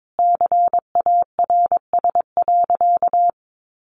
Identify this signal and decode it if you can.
Morse code, heard as a keyed tone, about 22 wpm, 700 Hz. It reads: CARS.
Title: Easy Phrases 22wpm